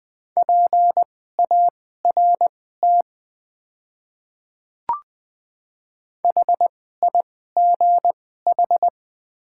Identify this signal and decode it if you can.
Morse code, heard as a keyed tone, about 20 wpm, spaced as for 15 wpm, 700 Hz. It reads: PART HIGH
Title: 2x Character Spacing - Top 500 Words - 20wpm